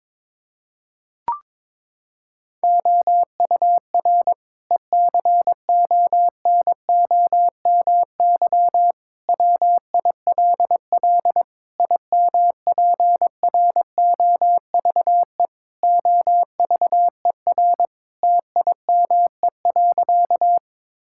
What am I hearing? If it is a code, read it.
Morse code, heard as a keyed tone, about 22 wpm, 700 Hz. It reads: OUR ECONOMY WILL IMPROVE OVER TIME.
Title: Sentences from Top 1000 Words - Review Part 5 - 22wpm